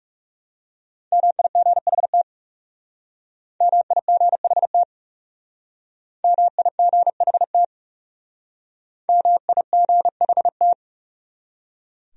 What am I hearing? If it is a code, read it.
Morse code, heard as a keyed tone, about 36 wpm, 700 Hz. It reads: MIGHT MIGHT MIGHT MIGHT
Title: Speed-Racing Top 500 Words 30wpm